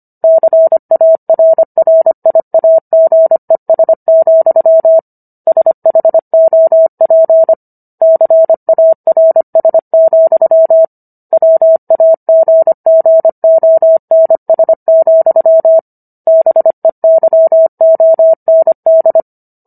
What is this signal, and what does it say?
Morse code, heard as a keyed tone, about 25 wpm, 650 Hz. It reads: CARRIAGES, SHOP CARS, WAGGONS, BEYOND